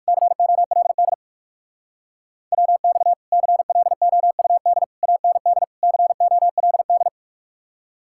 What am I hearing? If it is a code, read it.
Morse code, heard as a keyed tone, about 66 wpm, spaced as for 55 wpm, 700 Hz. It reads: COLD WX CLOUD AND COLD
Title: Speed-Racing Common QSO Phrases - 45wpm